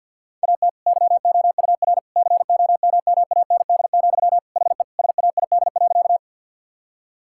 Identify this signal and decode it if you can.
Morse code, heard as a keyed tone, about 50 wpm, 700 Hz. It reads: AT YOUR COMMAND, HE SAID.
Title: Sentences from Top 1600 Words 50wpm